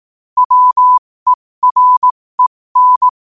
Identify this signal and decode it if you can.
Morse code, heard as a keyed tone, about 18 wpm, spaced as for 15 wpm, 1000 Hz. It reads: WEREN
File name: qotdfiles / 15wpm